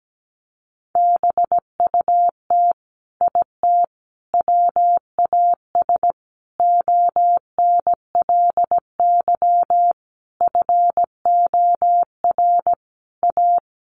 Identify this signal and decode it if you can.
Morse code, heard as a keyed tone, about 17 wpm, 700 Hz. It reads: BUT IT WAS ONLY FOR A
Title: Sentences from Top 1000 Words 17wpm